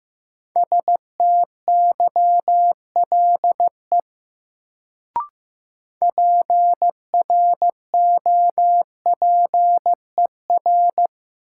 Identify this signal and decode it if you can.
Morse code, heard as a keyed tone, about 15 wpm, 700 Hz. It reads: STYLE PROPER
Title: Sentences from Top 800 Words - New Word Drill - 15wpm